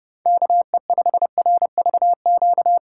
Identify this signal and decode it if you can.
Morse code, heard as a keyed tone, about 31 wpm, 700 Hz. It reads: KE5RVQ
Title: USA Call Signs - 6 Characters - 4x Character Spacing - 30wpm